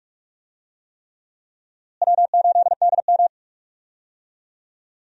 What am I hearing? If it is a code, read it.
Morse code, heard as a keyed tone, about 46 wpm, 700 Hz. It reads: W8DM
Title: USA Call Signs - 4 Characters - Repeated 3x - 45wpm